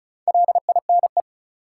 Morse code, heard as a keyed tone, about 36 wpm, 700 Hz. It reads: LINE
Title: Sentences from Top 1200 Words - New Word Drill Alt - 35wpm